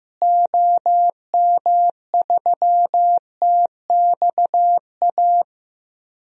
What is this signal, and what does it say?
Morse code, heard as a keyed tone, about 15 wpm, 700 Hz. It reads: OM3TXA
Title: DXCC-100 Call Signs - 6 Characters - Repeated 2x - 15wpm